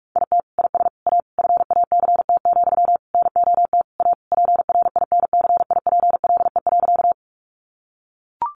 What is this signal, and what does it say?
Morse code, heard as a keyed tone, about 44 wpm, 700 Hz. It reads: IT IS A FACT, NOT A PRINCIPLE.
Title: Sentences from Top 1400 Words 45wpm